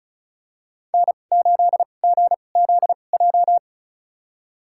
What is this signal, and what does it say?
Morse code, heard as a keyed tone, about 34 wpm, spaced as for 25 wpm, 700 Hz. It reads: N8GZJ
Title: USA Call Signs - 5 Characters - 2x Character Spacing - 35wpm